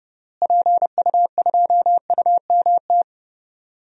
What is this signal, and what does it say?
Morse code, heard as a keyed tone, about 31 wpm, 700 Hz. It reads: PU2UMT
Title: DXCC-100 Call Signs - 6 Characters - Repeated 3x - 30wpm